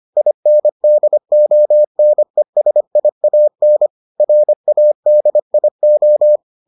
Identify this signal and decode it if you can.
Morse code, heard as a keyed tone, about 25 wpm, 600 Hz. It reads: INDONESIAN RADIO